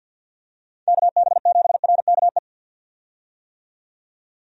Morse code, heard as a keyed tone, about 51 wpm, 700 Hz. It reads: KB7RKE
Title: USA Call Signs - 6 Characters - 50wpm